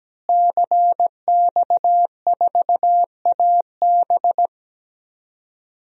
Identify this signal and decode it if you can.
Morse code, heard as a keyed tone, about 17 wpm, 700 Hz. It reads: CX4AB
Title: DXCC-100 Call Signs - 5 Characters - Repeated 3x - 17wpm